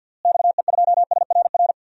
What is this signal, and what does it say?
Morse code, heard as a keyed tone, about 51 wpm, 700 Hz. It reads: XE2SRR
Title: DXCC-100 Call Signs - 6 Characters - Repeated 3x - 50wpm